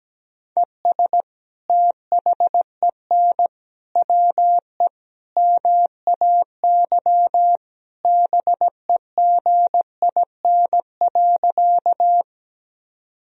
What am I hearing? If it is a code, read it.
Morse code, heard as a keyed tone, about 17 wpm, 700 Hz. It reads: ES THEN WE MAY BEGIN.
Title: Sentences from Top 300 Words and Abbreviations 17wpm